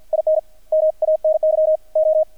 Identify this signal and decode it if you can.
Morse code, heard as a keyed tone, about 50 wpm, 630 Hz. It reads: IN MANY O
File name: archived / CRSnet